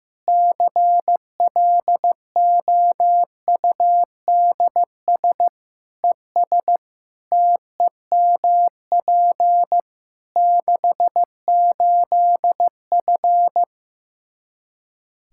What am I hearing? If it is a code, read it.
Morse code, heard as a keyed tone, about 15 wpm, 700 Hz. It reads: CLOUDS ES TEMP 68F
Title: Practice QSO - Exchange 2 - Info - 15wpm